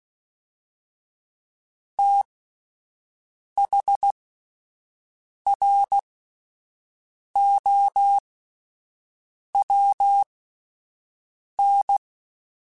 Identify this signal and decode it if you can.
Morse code, heard as a keyed tone, about 16 wpm, spaced as for 5 wpm, 800 Hz. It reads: THROWN